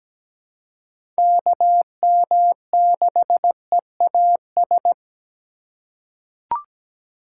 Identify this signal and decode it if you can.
Morse code, heard as a keyed tone, about 17 wpm, 700 Hz. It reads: KM6EAS
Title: USA Call Signs - 3x Character Spacing - 17wpm